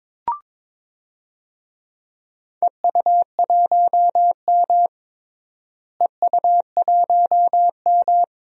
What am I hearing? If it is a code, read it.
Morse code, heard as a keyed tone, about 22 wpm, 700 Hz. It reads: EU1M EU1M
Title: DXCC-100 Call Signs - 4 Characters - Repeated 2x - 22wpm